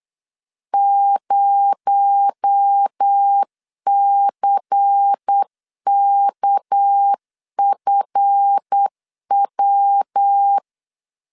Morse code, heard as a keyed tone, about 8 wpm, 800 Hz. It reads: 0CKFW
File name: LEZIONI / 8 YJ45 / 8 Y 8wpm 3245